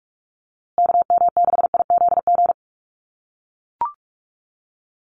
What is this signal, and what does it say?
Morse code, heard as a keyed tone, about 44 wpm, 700 Hz. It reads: KM6IZG
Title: USA Call Signs Repeated 3x 45wpm